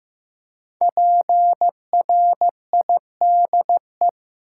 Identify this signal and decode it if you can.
Morse code, heard as a keyed tone, about 15 wpm, 700 Hz. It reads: PRIDE
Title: Five Letter Words - 15wpm